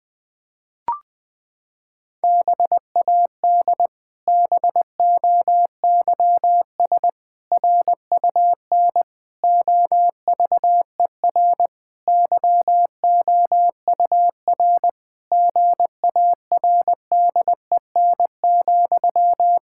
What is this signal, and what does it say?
Morse code, heard as a keyed tone, about 21 wpm, 700 Hz. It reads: BAD BOYS RUN OVER YOUR GARDEN,
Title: Pangram Challenge v2 - 20wpm